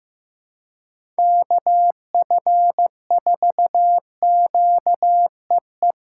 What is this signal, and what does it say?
Morse code, heard as a keyed tone, about 15 wpm, 700 Hz. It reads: KF4QEE
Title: USA Call Signs - 6x Character Spacing - 15wpm